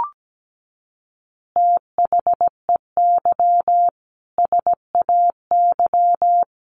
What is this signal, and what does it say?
Morse code, heard as a keyed tone, about 17 wpm, 700 Hz. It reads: THEY SAY
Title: Additional Easy Phrases 17wpm